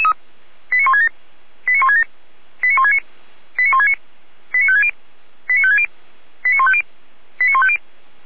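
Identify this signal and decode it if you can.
Five-tone selective call (ZVEI), heard as quick runs of five tones, tones of about 80 ms. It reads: 89167 89167 89169 89169 89560 89560 89150 89150
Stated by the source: ZVEI five-tone call received by SDR, from the feuerwehr-orlamuende.org fire brigade audio server.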